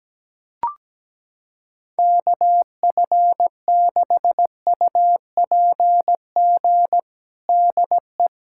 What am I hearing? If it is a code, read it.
Morse code, heard as a keyed tone, about 17 wpm, 700 Hz. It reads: KF6UPG DE
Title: USA Callsign DE Callsign - 6 Characters - 17wpm